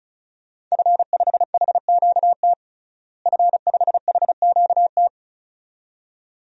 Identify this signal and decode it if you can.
Morse code, heard as a keyed tone, about 36 wpm, 700 Hz. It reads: F5HQT F5HQT
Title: DXCC-100 Call Signs - 5 Characters - Repeated 2x - 35wpm